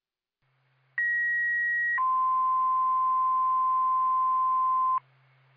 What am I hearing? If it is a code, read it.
Two-tone page, heard as a one-second tone, then a three-second tone, about 1900 then 1040 Hz. It is heard from 1.0 to 5.0 s.